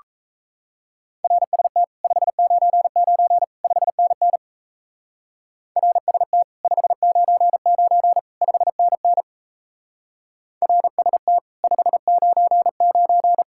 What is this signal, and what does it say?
Morse code, heard as a keyed tone, about 35 wpm, 700 Hz. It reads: RST 599 5NN RST 599 5NN RST 599
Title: Speed-Racing Common QSO Phrases - 28wpm